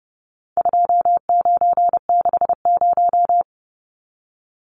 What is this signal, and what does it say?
Morse code, heard as a keyed tone, about 30 wpm, 700 Hz. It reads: 2960
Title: Numbers - 4-Digit 30wpm